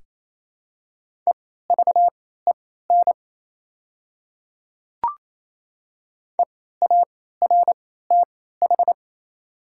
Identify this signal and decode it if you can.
Morse code, heard as a keyed tone, about 28 wpm, spaced as for 16 wpm, 700 Hz. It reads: EVEN EARTH
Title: 3x Character Spacing - Top 300 Words - 28wpm